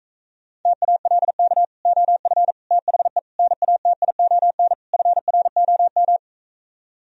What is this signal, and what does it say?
Morse code, heard as a keyed tone, about 45 wpm, 700 Hz. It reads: TALK OF THE NATION FROM